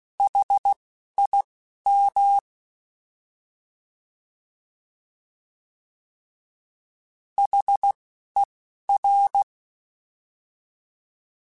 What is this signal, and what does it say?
Morse code, heard as a keyed tone, about 16 wpm, spaced as for 11 wpm, 800 Hz. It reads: HIM HER
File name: Alphabetical / Alphabetical mp3s / Hs @10